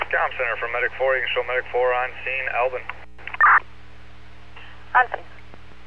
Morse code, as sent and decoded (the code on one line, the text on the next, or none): none